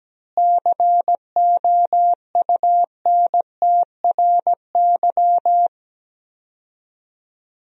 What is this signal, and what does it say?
Morse code, heard as a keyed tone, about 17 wpm, 700 Hz. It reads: COUNTRY
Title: 4x Word-Spacing - Sentences from Top 300 Words - 17wpm